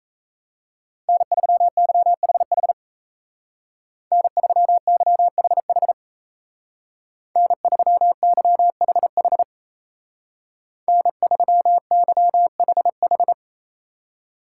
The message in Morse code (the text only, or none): N3YHH N3YHH N3YHH N3YHH